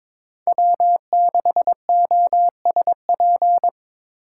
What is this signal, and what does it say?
Morse code, heard as a keyed tone, about 22 wpm, 700 Hz. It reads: W6OSP
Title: K1USN SST QSO Practice - 22wpm